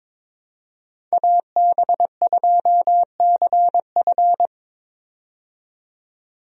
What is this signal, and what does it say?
Morse code, heard as a keyed tone, about 22 wpm, 700 Hz. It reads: AB2CF
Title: USA Call Signs - 5 Characters - 22wpm